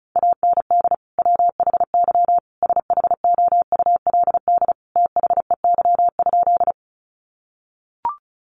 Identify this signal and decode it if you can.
Morse code, heard as a keyed tone, about 34 wpm, 700 Hz. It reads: AND WHY SHOULD THEY?
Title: Sentences from Top 300 Words 35wpm